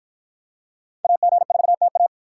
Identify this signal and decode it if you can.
Morse code, heard as a keyed tone, about 51 wpm, 700 Hz. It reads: AG4TA